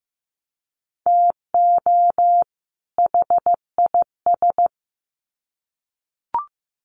Morse code, heard as a keyed tone, about 15 wpm, 700 Hz. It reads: TO HIS
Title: Sets of 2 Words 15wpm